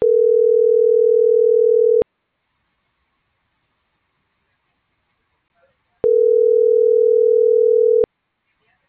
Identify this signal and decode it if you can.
Ringback tone (North America), 440 + 480 Hz.